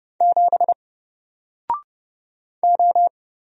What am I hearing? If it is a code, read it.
Morse code, heard as a keyed tone, about 30 wpm, 700 Hz. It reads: 7 O
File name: Single-Letter-Number-30wpm